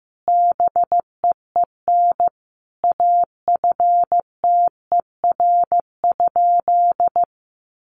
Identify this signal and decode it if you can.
Morse code, heard as a keyed tone, about 14 wpm, 700 Hz. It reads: BEEN AFTER?
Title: Sentences from Top 200 Words 15wpm